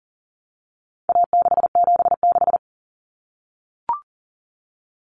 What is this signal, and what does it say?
Morse code, heard as a keyed tone, about 40 wpm, 700 Hz. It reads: A676
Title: Common Cut Numbers - 4-Digit 40wpm